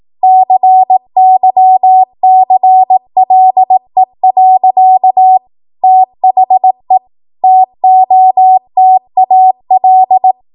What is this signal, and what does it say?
Morse code, heard as a keyed tone, about 18 wpm, 750 Hz. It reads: CYCLE. THE TOTAL
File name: Morse / 18WPM